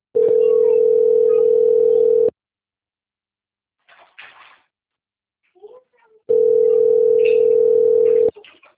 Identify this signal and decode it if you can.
Ringback tone (North America), 440 + 480 Hz, with speech.
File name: out- - - - - .25516